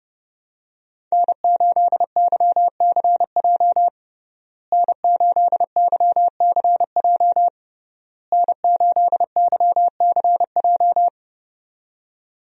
Morse code, heard as a keyed tone, about 32 wpm, 700 Hz. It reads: N8YCJ N8YCJ N8YCJ
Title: USA Call Signs - 5 Characters - Repeated 3x - 30wpm